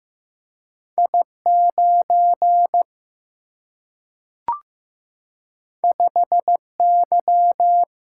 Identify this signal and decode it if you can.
Morse code, heard as a keyed tone, about 15 wpm, 700 Hz. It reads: I9 5Y
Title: Single-Letter Single-Number 15wpm